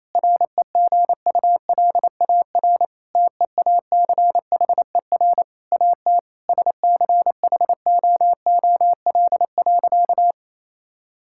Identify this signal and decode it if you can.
Morse code, heard as a keyed tone, about 28 wpm, 700 Hz. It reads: REGULAR TEACHER AT SCHOOL.
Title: Sentences from Top 1000 Words - Review Part 6 - 28wpm